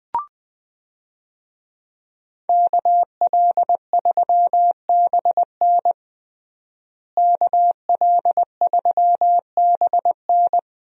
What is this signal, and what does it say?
Morse code, heard as a keyed tone, about 20 wpm, 700 Hz. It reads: KL3BN KL3BN
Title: USA Call Signs - 5 Characters - Repeated 3x - 20wpm